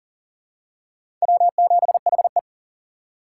Morse code, heard as a keyed tone, about 40 wpm, 700 Hz. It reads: W7HE
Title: USA Call Signs - 4 Characters - 2x Character Spacing - 40wpm